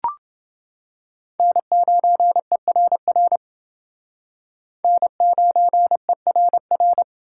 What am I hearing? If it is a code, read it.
Morse code, heard as a keyed tone, about 28 wpm, 700 Hz. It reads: N9ERR N9ERR